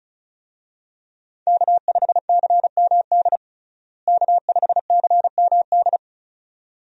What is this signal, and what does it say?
Morse code, heard as a keyed tone, about 34 wpm, 700 Hz. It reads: K5CMD K5CMD
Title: USA Call Signs - 5 Characters - Repeated 2x - 35wpm